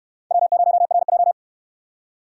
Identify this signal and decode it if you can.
Morse code, heard as a keyed tone, about 66 wpm, 700 Hz. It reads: W0RJ